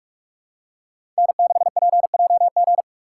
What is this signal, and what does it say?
Morse code, heard as a keyed tone, about 46 wpm, 700 Hz. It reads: N6PJG
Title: USA Call Signs - 5 Characters - 3x Character Spacing - 45wpm